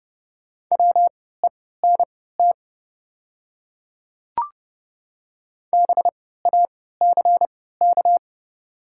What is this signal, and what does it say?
Morse code, heard as a keyed tone, about 30 wpm, spaced as for 17 wpm, 690 Hz. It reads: WENT BACK
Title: 3x Character Spacing - Top 200 Words - 30wpm